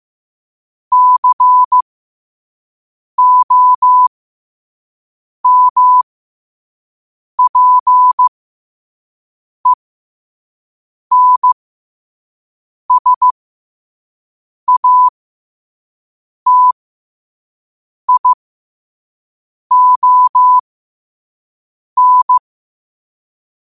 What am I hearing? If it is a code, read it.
Morse code, heard as a keyed tone, about 15 wpm, spaced as for 5 wpm, 1000 Hz. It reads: COMPENSATION
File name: qotdfiles / 5wpm